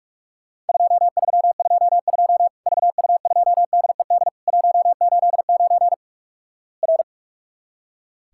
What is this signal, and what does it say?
Morse code, heard as a keyed tone, about 44 wpm, 700 Hz. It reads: 2322 VU2DED 189 R
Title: Worked All Europe DX Contest - QTC Exchange - 45wpm